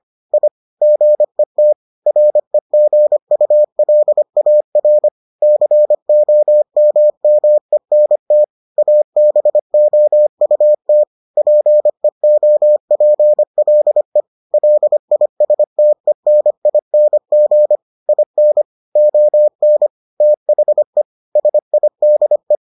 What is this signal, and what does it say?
Morse code, heard as a keyed tone, about 25 wpm, 600 Hz. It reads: I GET REGULAR COMMENT ABOUT PEOPLE LISTENING IN ON THE SIDE